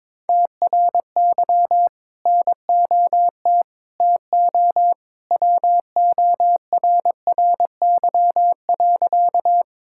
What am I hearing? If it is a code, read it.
Morse code, heard as a keyed tone, about 21 wpm, 700 Hz. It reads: TRY NOT TO WORRY.
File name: practice / 22wpm / Sentences with 2-Word Phrases - 22wpm